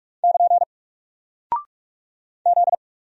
Morse code, heard as a keyed tone, about 45 wpm, 700 Hz. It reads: <KN> Z
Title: Single Character (Letter:Number:Punctuation:Prosign) Alt 45wpm